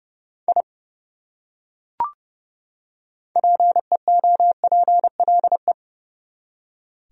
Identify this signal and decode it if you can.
Morse code, heard as a keyed tone, about 30 wpm, 700 Hz. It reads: I PEOPLE
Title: Top 100 Words 30wpm